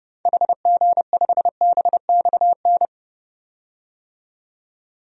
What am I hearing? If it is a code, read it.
Morse code, heard as a keyed tone, about 31 wpm, 700 Hz. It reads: HG5BXN
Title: DXCC-100 Call Signs - 6 Characters - 30wpm